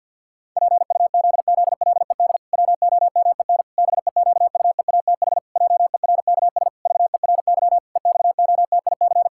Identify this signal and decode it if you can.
Morse code, heard as a keyed tone, about 50 wpm, 700 Hz. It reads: PUZZLED WOMEN BEQUEATH JERKS VERY EXOTIC
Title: Pangram Challenge v2 - 50wpm